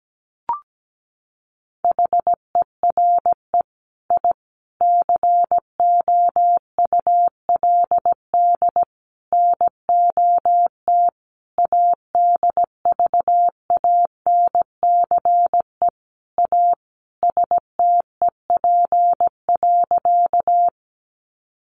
Morse code, heard as a keyed tone, about 17 wpm, 700 Hz. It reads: HERE I COULD NOT ADVANCE A STEP.
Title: Sentences from Top 1400 Words 17wpm